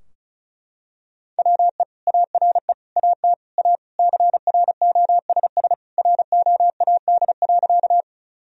Morse code, heard as a keyed tone, about 34 wpm, 700 Hz. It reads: WE ARE AT A CROSS ROAD.
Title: Sentences from Top 1000 Words - Review Part 2 - 35wpm